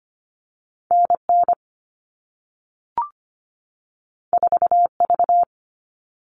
Morse code, heard as a keyed tone, about 25 wpm, 700 Hz. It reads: N N 4 V
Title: All Cut Numbers - 2-Digit 25wpm